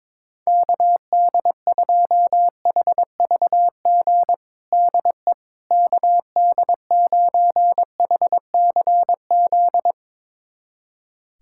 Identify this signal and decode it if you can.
Morse code, heard as a keyed tone, about 22 wpm, 700 Hz. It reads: KD2HVG DE KD9HCZ
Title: USA Callsign DE Callsign - 6 Characters - 22wpm